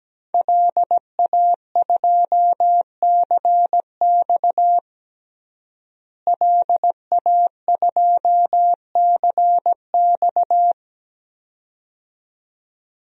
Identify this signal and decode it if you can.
Morse code, heard as a keyed tone, about 17 wpm, 700 Hz. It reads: LA2CX LA2CX
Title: DXCC-100 Call Signs - 5 Characters - Repeated 2x - 17wpm